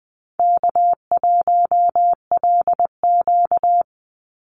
Morse code, heard as a keyed tone, about 20 wpm, 700 Hz. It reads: K1LQ
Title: USA Call Signs 20wpm